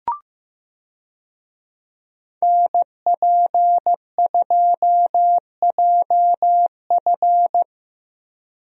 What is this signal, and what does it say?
Morse code, heard as a keyed tone, about 15 wpm, 700 Hz. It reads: NP2JF
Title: USA Call Signs - 5 Characters - Repeated 2x - 15wpm